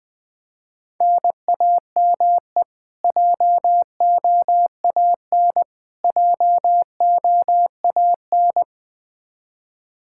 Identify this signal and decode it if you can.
Morse code, heard as a keyed tone, about 20 wpm, 700 Hz. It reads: NAME JOAN JOAN
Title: Practice QSO - Exchange 1 - Beginning - 20wpm